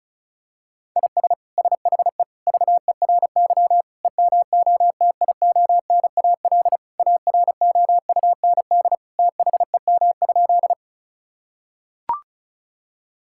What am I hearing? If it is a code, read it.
Morse code, heard as a keyed tone, about 36 wpm, 700 Hz. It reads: IS SHE VERY EMOTIONAL AROUND THEM?
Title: Sentences from Top 1000 Words - Review Part 5 - 35wpm